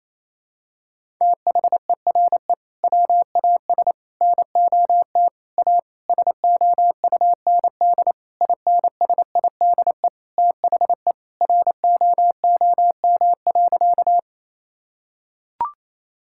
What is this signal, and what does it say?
Morse code, heard as a keyed tone, about 28 wpm, 700 Hz. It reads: THERE WAS NOT A SOUND INSIDE THE ROOM.
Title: Sentences from Top 1000 Words - Review Part 6 - 28wpm